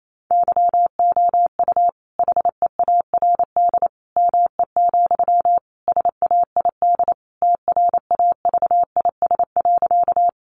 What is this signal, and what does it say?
Morse code, heard as a keyed tone, about 28 wpm, 700 Hz. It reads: YOU HEARD ME, SAID TRAVIS.
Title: A Sound Of Thunder 28wpm